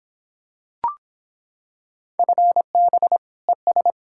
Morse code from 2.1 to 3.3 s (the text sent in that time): FB